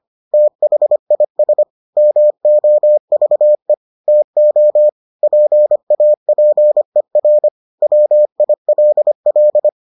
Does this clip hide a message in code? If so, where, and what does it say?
Yes: Morse code from 0.3 to 9.7 s, heard as a keyed tone, about 25 wpm, 600 Hz. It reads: THIS MOVE TO PAPER WILL